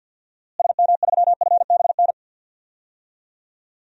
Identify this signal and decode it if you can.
Morse code, heard as a keyed tone, about 53 wpm, 700 Hz. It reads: SM3FBN
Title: DXCC-100 Call Signs - 6 Characters - Repeated 2x - 50wpm